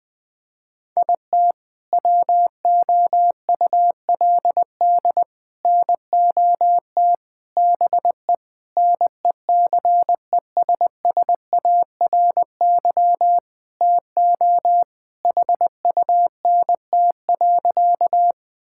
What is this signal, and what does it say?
Morse code, heard as a keyed tone, about 21 wpm, 700 Hz. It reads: IT WOULD NOT BE NECESSARY TO HUNT.
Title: Sentences from Top 1600 Words 20wpm